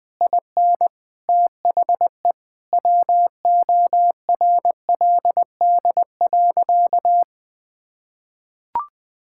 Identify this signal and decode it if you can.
Morse code, heard as a keyed tone, about 20 wpm, 700 Hz. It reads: IN THE WORLD.
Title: Sentences with 7-Word Phrases - Spoken Beginning - 20wpm